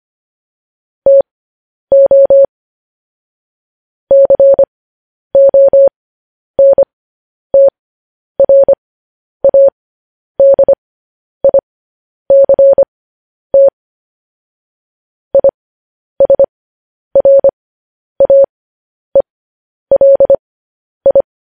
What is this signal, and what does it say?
Morse code, heard as a keyed tone, about 25 wpm, spaced as for 10 wpm, 550 Hz. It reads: TO CONTRADICT ISRAELI
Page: news headlines in morse code at 10 wpm